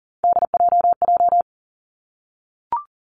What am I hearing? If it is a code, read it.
Morse code, heard as a keyed tone, about 41 wpm, 700 Hz. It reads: DJJ
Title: Three Challenging Letter-Number Combinations 40wpm